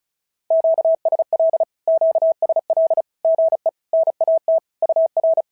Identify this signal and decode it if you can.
Morse code, heard as a keyed tone, about 36 wpm, 650 Hz. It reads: QSL QSL GE NAT UR
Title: SOTA QSO Practice 35wpm